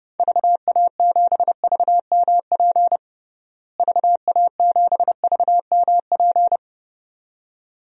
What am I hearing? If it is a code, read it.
Morse code, heard as a keyed tone, about 29 wpm, 700 Hz. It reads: VA7VMP VA7VMP
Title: DXCC-100 Call Signs - 6 Characters - Repeated 2x - 30wpm